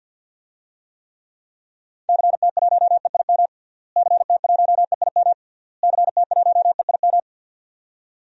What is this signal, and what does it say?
Morse code, heard as a keyed tone, about 53 wpm, 700 Hz. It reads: CT1EIM CT1EIM CT1EIM
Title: DXCC-100 Call Signs - 6 Characters - Repeated 3x - 50wpm